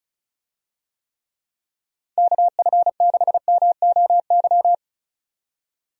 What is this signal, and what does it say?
Morse code, heard as a keyed tone, about 36 wpm, 700 Hz. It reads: KF6MOY
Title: USA Call Signs - 6 Characters - 35wpm